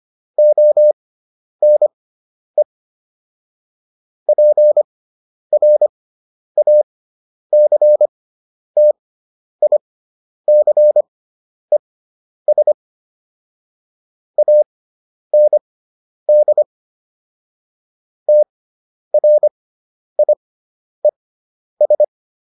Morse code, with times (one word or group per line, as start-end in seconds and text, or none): 0.4-2.6 ONE
4.3-12.7 PRACTICES
14.4-16.6 AND
18.3-22.0 TRIES